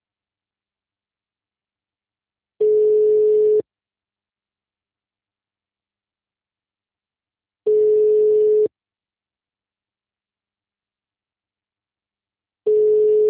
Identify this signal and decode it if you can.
Ringback tone (Europe, 425 Hz).